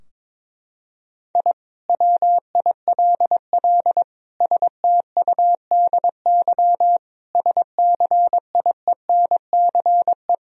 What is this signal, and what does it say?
Morse code, heard as a keyed tone, about 22 wpm, 700 Hz. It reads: I WILL STUDY SCIENCE
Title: Sentences from Top 1000 Words - Review Part 2 - 22wpm